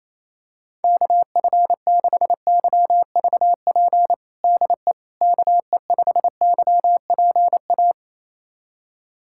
Morse code, heard as a keyed tone, about 28 wpm, 700 Hz. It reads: KF6YVP DE KE5YPA
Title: USA Callsign DE Callsign 28wpm